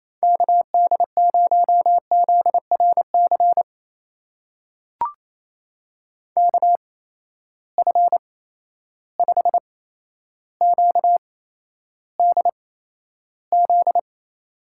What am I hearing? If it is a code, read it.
Morse code, heard as a keyed tone, about 28 wpm, spaced as for 12 wpm, 700 Hz. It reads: KD0ZRC KF5QDZ
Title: USA Call Signs - 6 Characters - 8x Character Spacing - 28wpm